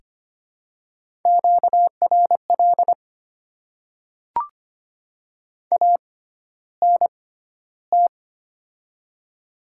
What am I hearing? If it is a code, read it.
Morse code, heard as a keyed tone, about 25 wpm, spaced as for 13 wpm, 700 Hz. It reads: QRL ANT
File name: Top 20 QSO Elements - 6x Character Spacing - 25wpm